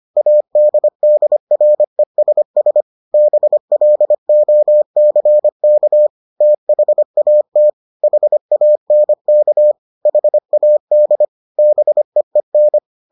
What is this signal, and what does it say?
Morse code, heard as a keyed tone, about 25 wpm, 600 Hz. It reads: ADDRESS BLOCK THAT HANK HAD BEEN